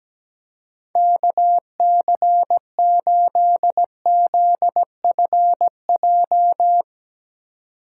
Morse code, heard as a keyed tone, about 17 wpm, 700 Hz. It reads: KC8ZFJ
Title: USA Call Signs - 4x Character Spacing - 17wpm